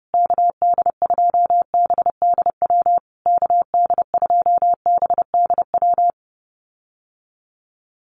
Morse code, heard as a keyed tone, about 30 wpm, 700 Hz. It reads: KD2BDW KD2BDW
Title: USA Call Signs Repeated 3x 30wpm